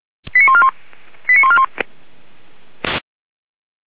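Five-tone selective call (ZVEI), heard as quick runs of five tones, tones of about 70 ms. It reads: 89141 89141
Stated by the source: Five-tone ZVEI selective call received by SDR, from the audio archive of feuerwehr-orlamuende.org (fire brigade).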